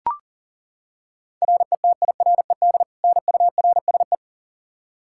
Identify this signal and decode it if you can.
Morse code, heard as a keyed tone, about 40 wpm, 700 Hz. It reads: RETIRED NURSE
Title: Practice QSO - Exchange 3 - More Info - 40wpm